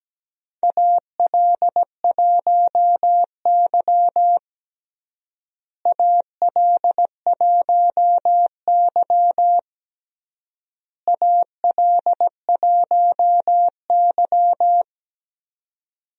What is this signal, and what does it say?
Morse code, heard as a keyed tone, about 17 wpm, 700 Hz. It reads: AL1Y AL1Y AL1Y
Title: USA Call Signs - 4 Characters - Repeated 3x - 17wpm